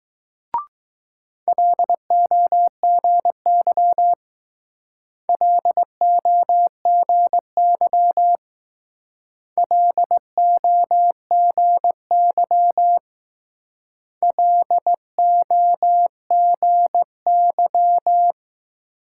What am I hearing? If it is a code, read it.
Morse code, heard as a keyed tone, about 20 wpm, 700 Hz. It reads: LOGY LOGY LOGY LOGY